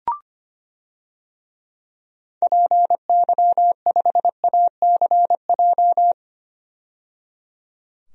Morse code, heard as a keyed tone, about 25 wpm, 700 Hz. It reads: PY5ACJ
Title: DXCC-100 Call Signs - 6 Characters - 25wpm